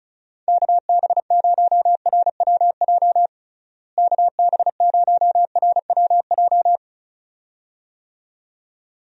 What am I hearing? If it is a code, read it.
Morse code, heard as a keyed tone, about 36 wpm, 700 Hz. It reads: KB0RWJ KB0RWJ
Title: USA Call Signs - 6 Characters - Repeated 2x - 35wpm